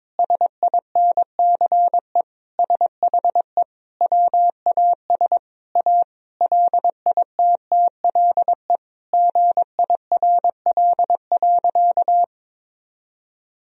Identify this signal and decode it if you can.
Morse code, heard as a keyed tone, about 22 wpm, 700 Hz. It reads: SINCE SHE WAS A LITTLE GIRL.
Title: Sentences from Top 1000 Words - Review Part 1 - 22wpm